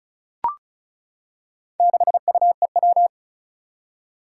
Morse code, heard as a keyed tone, about 36 wpm, 690 Hz. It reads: 6UEW
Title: Four Challenging Letter-Number Combinations 35wpm